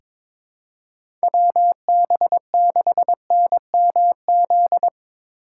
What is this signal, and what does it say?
Morse code, heard as a keyed tone, about 22 wpm, 700 Hz. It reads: WB6NMZ
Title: USA Call Signs - 6 Characters - 3x Character Spacing - 22wpm